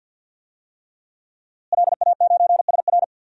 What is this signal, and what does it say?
Morse code, heard as a keyed tone, about 52 wpm, 700 Hz. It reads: LA9SR
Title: DXCC-100 Call Signs - 5 Characters - Repeated 3x - 50wpm